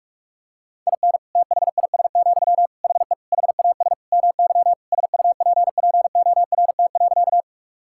Morse code, heard as a keyed tone, about 46 wpm, 700 Hz. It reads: IN THIS, HE HAS MY SUPPORT.
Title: Sentences from Top 1000 Words - Review Part 5 - 45wpm